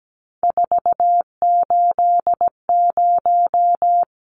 Morse code, heard as a keyed tone, about 17 wpm, 700 Hz. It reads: 480
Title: Numbers - 3-Digit 17wpm